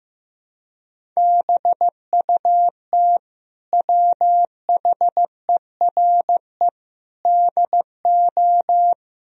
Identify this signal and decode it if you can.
Morse code, heard as a keyed tone, about 15 wpm, 700 Hz. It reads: BUT WHERE DO
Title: Sentences from Top 1000 Words - Review Part 1 - 15wpm